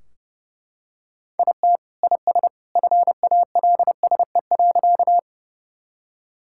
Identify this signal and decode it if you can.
Morse code, heard as a keyed tone, about 30 wpm, 700 Hz. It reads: IT IS FALSE.
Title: Sentences from Top 1600 Words 30wpm